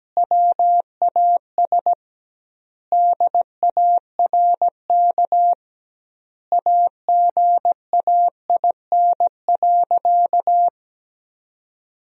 Morse code, heard as a keyed tone, about 17 wpm, 700 Hz. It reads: WAS DARK AGAIN.
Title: 2x Word-Spacing - Sentences from Top 500 Words - 17wpm